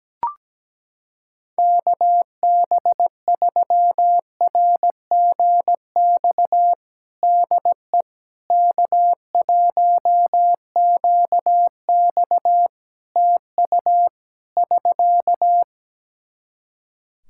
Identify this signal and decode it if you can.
Morse code, heard as a keyed tone, about 17 wpm, 700 Hz. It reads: KB3RGX DE K1QX TU <SK>
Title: Practice QSO - Exchange 4 - Ending - 17wpm